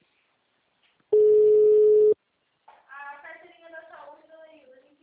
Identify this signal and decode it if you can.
Ringback tone (Europe, 425 Hz), with speech.